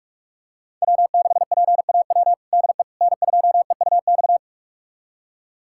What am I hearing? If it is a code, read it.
Morse code, heard as a keyed tone, about 46 wpm, 700 Hz. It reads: W6PAW DE N2EUX
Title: USA Callsign DE Callsign - 5 Characters - 45wpm